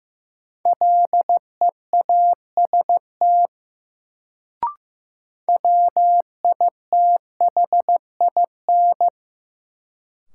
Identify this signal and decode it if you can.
Morse code, heard as a keyed tone, about 16 wpm, 700 Hz. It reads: LEAST WITHIN
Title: Sentences from Top 700 Words - New Word Drill - 15wpm